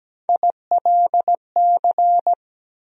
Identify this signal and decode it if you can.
Morse code, heard as a keyed tone, about 17 wpm, 700 Hz. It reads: ILC
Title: USA Call Sign Postfixes - 3 Letters 17wpm